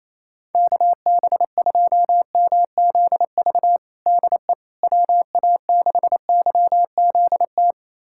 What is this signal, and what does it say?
Morse code, heard as a keyed tone, about 28 wpm, 700 Hz. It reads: KB2MZV DE WA6YZT
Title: USA Callsign DE Callsign - 6 Characters - 28wpm